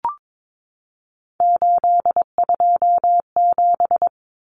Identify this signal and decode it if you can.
Morse code, heard as a keyed tone, about 22 wpm, 700 Hz. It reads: 827
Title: Common Cut Numbers - 3-Digit 22wpm